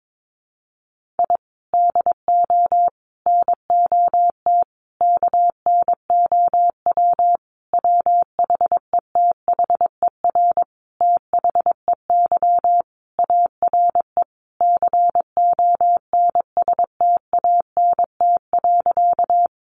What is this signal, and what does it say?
Morse code, heard as a keyed tone, about 22 wpm, 700 Hz. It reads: I DO NOT KNOW WHETHER THEY ARE CONSTANT.
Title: Sentences from Top 1500 Words 22wpm